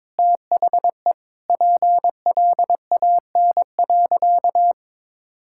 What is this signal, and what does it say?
Morse code, heard as a keyed tone, about 21 wpm, 700 Hz. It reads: THE PLAN.
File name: practice / 22wpm / Sentences with 4-Word Phrases - 22wpm